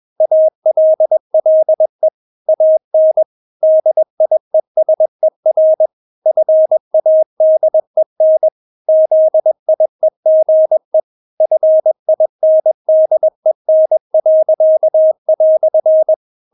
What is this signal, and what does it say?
Morse code, heard as a keyed tone, about 21 wpm, 630 Hz. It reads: ALLE AN DIESER FADEN ZIEGE FINDEN."
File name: Witz2 21wpm